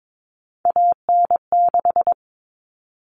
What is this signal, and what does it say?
Morse code, heard as a keyed tone, about 22 wpm, 700 Hz. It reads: AN6
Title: Common Cut Numbers - 3-Digit 22wpm